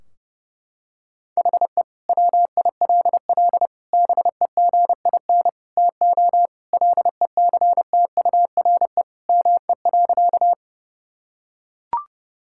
Morse code, heard as a keyed tone, about 30 wpm, 700 Hz. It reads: HE WILL BEGIN TO LECTURE ME.
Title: Sentences from Top 1600 Words 30wpm